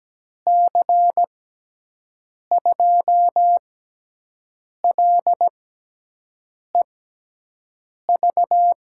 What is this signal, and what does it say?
Morse code, heard as a keyed tone, about 17 wpm, spaced as for 6 wpm, 700 Hz. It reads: C2LEV
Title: USA Call Signs - 6 Characters - 6x Character Spacing - 17wpm